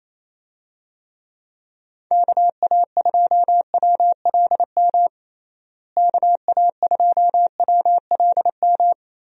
Morse code, heard as a keyed tone, about 28 wpm, 700 Hz. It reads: KA2WLM KA2WLM
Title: USA Call Signs - 6 Characters - Repeated 3x - 28wpm